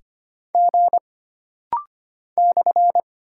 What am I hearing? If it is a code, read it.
Morse code, heard as a keyed tone, about 25 wpm, 700 Hz. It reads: G /
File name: Single Character (Letter:Number:Punctuation:Prosign) - Rapid-Fire - 25wpm